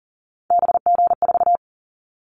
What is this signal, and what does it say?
Morse code, heard as a keyed tone, about 40 wpm, 700 Hz. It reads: BG4
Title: Callsign Prefixes for Top 100 DXCC - 40wpm